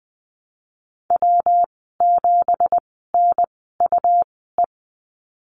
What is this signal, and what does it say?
Morse code, heard as a keyed tone, about 19 wpm, spaced as for 14 wpm, 700 Hz. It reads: W7NUE
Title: 186 Lesson 30g - Letter b - Call Signs 20wpm